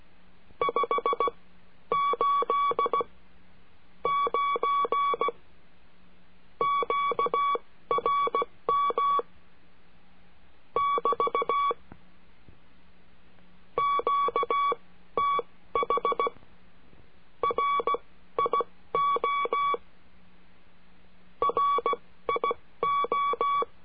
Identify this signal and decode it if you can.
Morse code, heard as a keyed tone, about 16 wpm, spaced as for 12 wpm, 1130 Hz. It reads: 589 QRM <BT> QTH RIO RIO
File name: cursos / CURSO DE TELEGRAFIA / Telegrafia - Trilha 22